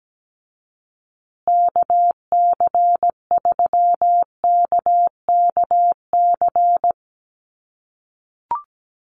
Morse code, heard as a keyed tone, about 17 wpm, 700 Hz. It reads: KC3KKC
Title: USA Call Signs Repeated 3x 17wpm